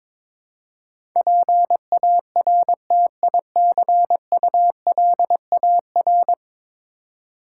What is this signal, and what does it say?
Morse code, heard as a keyed tone, about 22 wpm, 700 Hz. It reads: PARTICULAR
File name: practice / 22wpm / 10 Letter Words - First 3 Letters Given - 22wpm